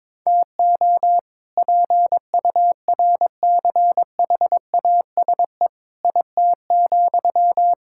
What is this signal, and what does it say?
Morse code, heard as a keyed tone, about 22 wpm, 700 Hz. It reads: TO PURCHASE IT,
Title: Sentences from Top 1000 Words - Review Part 5 - 22wpm